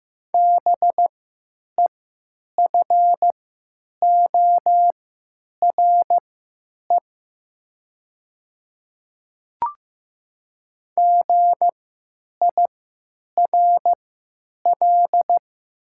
Morse code, heard as a keyed tone, about 15 wpm, spaced as for 9 wpm, 700 Hz. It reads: BEFORE GIRL
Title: 3x Character Spacing - Top 500 Words - 15wpm